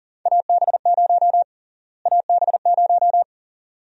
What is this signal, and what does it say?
Morse code, heard as a keyed tone, about 41 wpm, 700 Hz. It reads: AB0 AB0
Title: USA Call Sign Prefixes - Repeated 2x - 40wpm